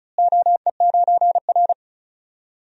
Morse code, heard as a keyed tone, about 36 wpm, 700 Hz. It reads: OE9R
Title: DXCC-100 Call Signs - 4 Characters - Repeated 3x - 35wpm